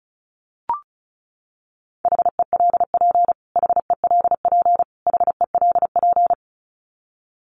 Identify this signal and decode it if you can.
Morse code, heard as a keyed tone, about 34 wpm, 700 Hz. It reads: HELP HELP HELP
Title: Top 500 Words Repeated 35wpm